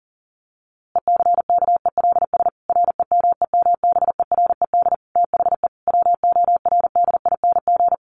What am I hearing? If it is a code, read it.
Morse code, heard as a keyed tone, about 40 wpm, 700 Hz. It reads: ECKELS REMEMBERED THE WORDING
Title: A Sound Of Thunder 40wpm